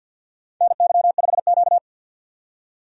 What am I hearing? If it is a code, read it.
Morse code, heard as a keyed tone, about 47 wpm, 700 Hz. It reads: NY5Q
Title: USA Call Signs - 4 Characters - 50wpm